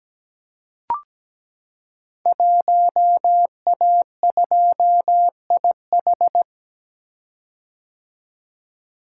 Morse code, heard as a keyed tone, about 17 wpm, 700 Hz. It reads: 1A2IH
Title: Code Groups - 5 Letter:Numbers - 17wpm